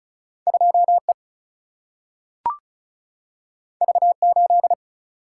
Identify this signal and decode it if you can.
Morse code, heard as a keyed tone, about 36 wpm, 700 Hz. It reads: 2 E V 8
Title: Single-Letter Single-Number 35wpm